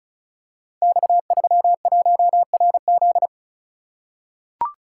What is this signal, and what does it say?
Morse code, heard as a keyed tone, about 36 wpm, 700 Hz. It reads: X31RZ
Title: Code Groups - 5 Letter:Numbers - 35wpm